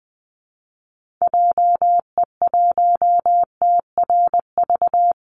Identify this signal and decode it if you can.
Morse code, heard as a keyed tone, about 20 wpm, 700 Hz. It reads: JE1TRV
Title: Call Signs Repeated 2x - 20wpm